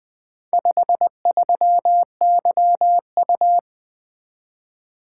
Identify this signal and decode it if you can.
Morse code, heard as a keyed tone, about 20 wpm, 700 Hz. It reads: 53YU
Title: Four Letter-Number Combinations - 20wpm